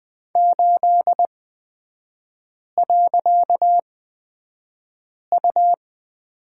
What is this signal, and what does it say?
Morse code, heard as a keyed tone, about 20 wpm, spaced as for 5 wpm, 700 Hz. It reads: 8.U
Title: 255 Lesson 41e - Period - Three Characters 20wpm